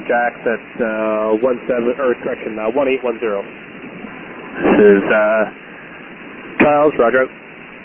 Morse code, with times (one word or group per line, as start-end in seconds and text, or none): none